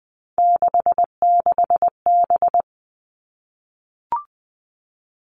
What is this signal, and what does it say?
Morse code, heard as a keyed tone, about 20 wpm, 700 Hz. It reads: 66B
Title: All Cut Numbers - 3-Digit 20wpm